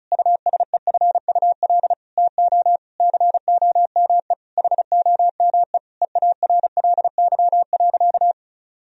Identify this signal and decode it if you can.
Morse code, heard as a keyed tone, about 34 wpm, 700 Hz. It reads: USEFUL TO COME HOME EARLY.
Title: Sentences from Top 1000 Words - Review Part 4 - 35wpm